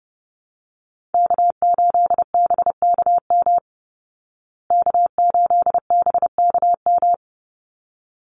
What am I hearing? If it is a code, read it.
Morse code, heard as a keyed tone, about 30 wpm, 700 Hz. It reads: K8BKM K8BKM
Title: Call Signs Repeated 3x - 30wpm